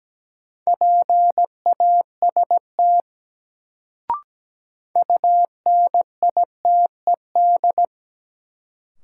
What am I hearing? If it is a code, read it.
Morse code, heard as a keyed tone, about 17 wpm, 700 Hz. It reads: PAST UNITED
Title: Sets of 2 Words - New Word Drill - 17wpm